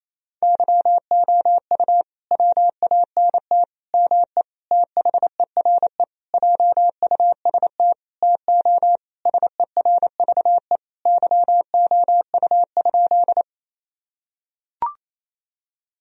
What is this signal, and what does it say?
Morse code, heard as a keyed tone, about 28 wpm, 700 Hz. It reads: YOU WANT ME THERE JUST TO SERVE YOU?
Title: Sentences from Top 1000 Words - Review Part 2 - 28wpm